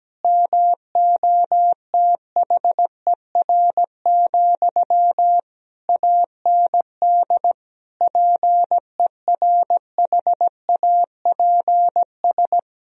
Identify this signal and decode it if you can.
Morse code, heard as a keyed tone, about 17 wpm, 700 Hz. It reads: MOTHER, AND PERHAPS
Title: Sentences from Top 1000 Words - Review Part 1 - 17wpm